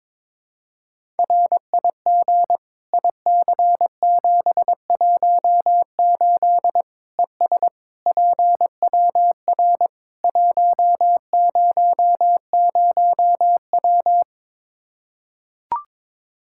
Morse code, heard as a keyed tone, about 22 wpm, 700 Hz. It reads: RIG IC718 ES PWR 100W
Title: Practice QSO - Exchange 2 - Info - 22wpm